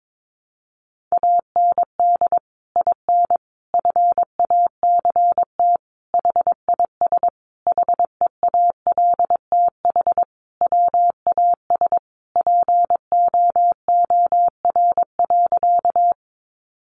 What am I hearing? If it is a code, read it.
Morse code, heard as a keyed tone, about 22 wpm, 700 Hz. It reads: AND IN FACT HIS HEALTH WAS POOR.
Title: Sentences from Top 1000 Words 22wpm